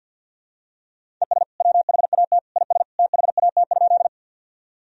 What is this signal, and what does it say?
Morse code, heard as a keyed tone, about 49 wpm, 700 Hz. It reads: ES WHAT IS THAT?
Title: Sentences from Top 100 Words and Abbreviations 50wpm